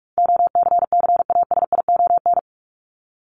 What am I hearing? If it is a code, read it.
Morse code, heard as a keyed tone, about 44 wpm, 700 Hz. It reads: OCCASION
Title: Top 1501-1750 Words 45wpm